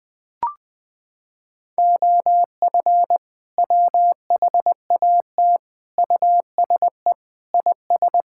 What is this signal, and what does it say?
Morse code, heard as a keyed tone, about 20 wpm, 700 Hz. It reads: OF WHAT USE IS
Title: Sentences from Top 1000 Words - Review Part 5 - 20wpm